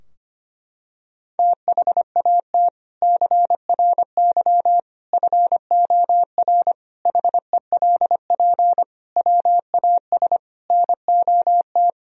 Morse code, heard as a keyed tone, about 25 wpm, 700 Hz. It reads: THAT CRY FOR HELP WAS NOT
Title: Sentences from Top 1000 Words - Review Part 3 - 25wpm